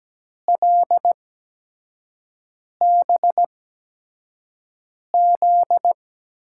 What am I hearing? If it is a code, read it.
Morse code, heard as a keyed tone, about 17 wpm, spaced as for 5 wpm, 700 Hz. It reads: LBZ
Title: USA Call Signs - 8x Character Spacing - 17wpm